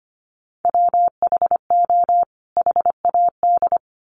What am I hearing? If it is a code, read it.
Morse code, heard as a keyed tone, about 25 wpm, 700 Hz. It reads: WHO HAD
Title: Sets of 2 Words 25wpm